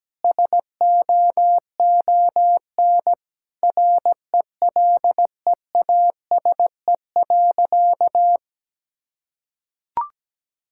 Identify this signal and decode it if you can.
Morse code, heard as a keyed tone, about 17 wpm, 700 Hz. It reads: SOON RELEASE.
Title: Sentences from Top 1600 Words 17wpm